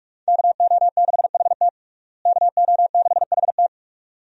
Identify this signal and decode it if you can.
Morse code, heard as a keyed tone, about 46 wpm, 700 Hz. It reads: KO6HT KO6HT
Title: USA Call Signs - 5 Characters - Repeated 2x - 45wpm